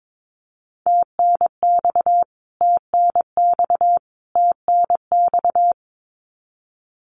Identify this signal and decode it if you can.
Morse code, heard as a keyed tone, about 22 wpm, 700 Hz. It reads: TNX TNX TNX
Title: Abbreviations Repeated 3x - 22wpm